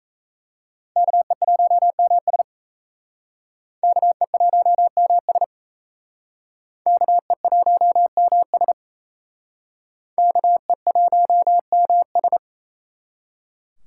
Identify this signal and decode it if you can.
Morse code, heard as a keyed tone, about 36 wpm, 700 Hz. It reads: KE1MS KE1MS KE1MS KE1MS